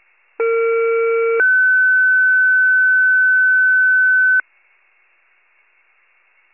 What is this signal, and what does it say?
Two-tone page, heard as a one-second tone, then a three-second tone, about 460 then 1500 Hz. It is heard from 0.4 to 4.4 s.
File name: pager / 21 FIRE / wcalert